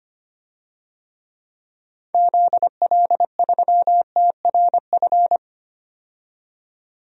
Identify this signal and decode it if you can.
Morse code, heard as a keyed tone, about 25 wpm, 700 Hz. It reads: ZL3TRF
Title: DXCC-100 Call Signs - 6 Characters - 25wpm